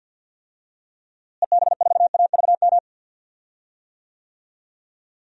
Morse code, heard as a keyed tone, about 53 wpm, 700 Hz. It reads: EB4AVM
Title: DXCC-100 Call Signs - 6 Characters - Repeated 2x - 50wpm